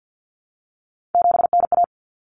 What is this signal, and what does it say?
Morse code, heard as a keyed tone, about 50 wpm, 700 Hz. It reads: 7NA
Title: Common Cut Numbers - 3-Digit 50wpm